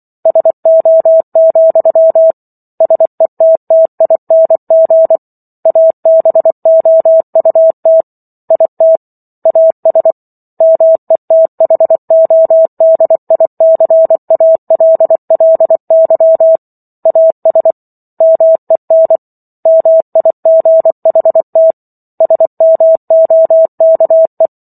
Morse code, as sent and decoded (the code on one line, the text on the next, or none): ... --- --..-- / ... - - .. -. --. / .- -... --- ..- - / .. - / .- ... / -- . - .... --- -.. .. -.-. .- .-.. .-.. -.-- / .- ... / -- . -. / -- .. --. .... - / ... -- --- -.- .
SO, SETTING ABOUT IT AS METHODICALLY AS MEN MIGHT SMOKE